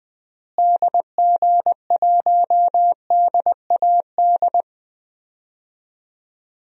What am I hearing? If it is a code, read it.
Morse code, heard as a keyed tone, about 21 wpm, 700 Hz. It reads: DG1DAD
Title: DXCC-100 Call Signs - 6 Characters - Repeated 2x - 20wpm